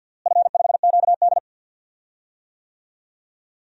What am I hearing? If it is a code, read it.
Morse code, heard as a keyed tone, about 47 wpm, 700 Hz. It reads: F5QD
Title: DXCC-100 Call Signs - 4 Characters - Repeated 2x - 50wpm